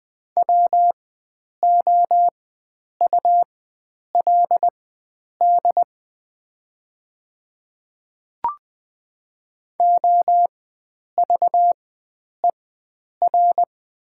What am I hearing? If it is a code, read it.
Morse code, heard as a keyed tone, about 20 wpm, spaced as for 9 wpm, 700 Hz. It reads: WOULD OVER
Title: 4x Character Spacing - Top 100 Words - 20wpm